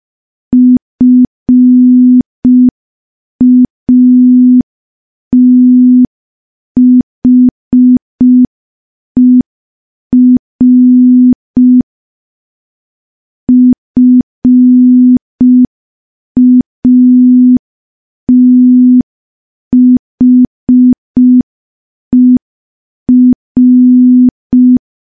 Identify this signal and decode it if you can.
Morse code, heard as a keyed tone, about 5 wpm, 250 Hz. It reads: FATHER FATHER
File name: morse / morse-training-05WPM